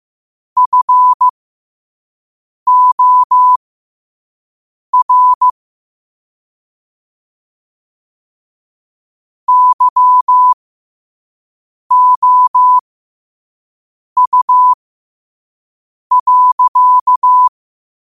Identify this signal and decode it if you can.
Morse code, heard as a keyed tone, about 15 wpm, spaced as for 5 wpm, 1000 Hz. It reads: FOR YOU.
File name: qotdfiles / 5wpm